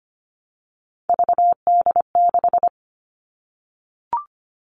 Morse code, heard as a keyed tone, about 25 wpm, 700 Hz. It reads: VD6
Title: All Cut Numbers - 3-Digit 25wpm